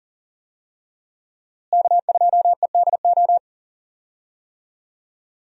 Morse code, heard as a keyed tone, about 38 wpm, 700 Hz. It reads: K2EDO
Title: USA Call Signs - 5 Characters - Repeated 3x - 40wpm